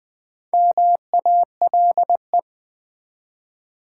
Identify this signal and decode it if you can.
Morse code, heard as a keyed tone, about 20 wpm, 700 Hz. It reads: MALE
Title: Four Letter Words - 20wpm